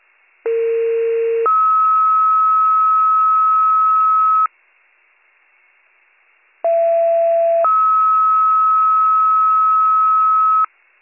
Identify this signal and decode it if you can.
Two-tone page, heard as a one-second tone, then a three-second tone, about 460 then 1290 Hz, about 670 then 1290 Hz. It is heard from 0.5 to 10.7 s.